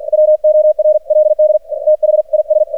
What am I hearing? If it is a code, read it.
Morse code, heard as a keyed tone, about 49 wpm, 610 Hz. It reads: WORLD THIS
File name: archived / CRSnet